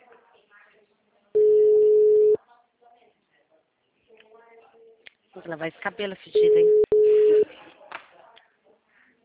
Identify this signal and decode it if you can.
Ringback tone (Europe, 425 Hz), with speech.
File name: out- -2009- - - .89563